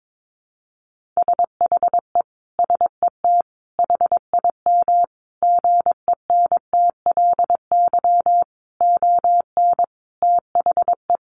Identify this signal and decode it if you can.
Morse code, heard as a keyed tone, about 22 wpm, 700 Hz. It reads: SHE SET HIM GENTLY ON THE
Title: Sentences from Top 1500 Words 22wpm